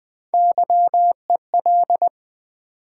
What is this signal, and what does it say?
Morse code, heard as a keyed tone, about 20 wpm, 700 Hz. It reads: YEL
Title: Build A Word - 1 Letter At A Time - 4-Letter Words - 20wpm